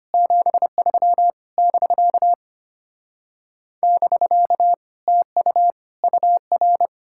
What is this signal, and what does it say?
Morse code, heard as a keyed tone, about 33 wpm, 700 Hz. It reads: 73 <BK> <BK> TU UR